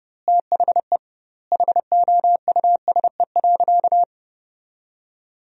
Morse code, heard as a keyed tone, about 32 wpm, 700 Hz. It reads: THE HOUSE.
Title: 2x Word-Spacing - Sentences from Top 500 Words - 30wpm